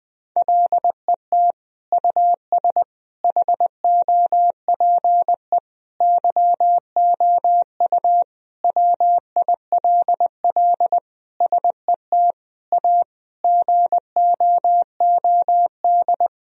Morse code, heard as a keyed tone, about 20 wpm, 700 Hz. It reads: LET US HOPE YOU WILL SET A GOOD
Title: Sentences from Top 1000 Words - Review Part 1 - 20wpm